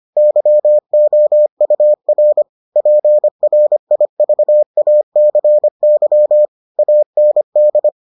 Morse code, heard as a keyed tone, about 25 wpm, 600 Hz. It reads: YOUR PRIVACY AND